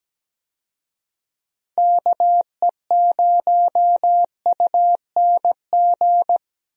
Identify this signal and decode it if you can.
Morse code, heard as a keyed tone, about 17 wpm, 700 Hz. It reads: KE0UNG
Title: USA Call Signs - 6 Characters - 17wpm